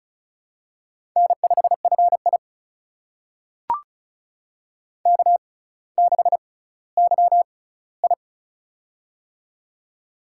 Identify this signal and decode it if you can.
Morse code, heard as a keyed tone, about 36 wpm, spaced as for 18 wpm, 700 Hz. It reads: N5FI K6YI
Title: USA Call Signs - 4 Characters - 6x Character Spacing - 35wpm